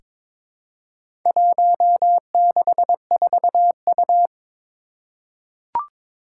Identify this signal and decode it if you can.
Morse code, heard as a keyed tone, about 22 wpm, 700 Hz. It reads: 164U
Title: Four Challenging Letter-Number Combinations 22wpm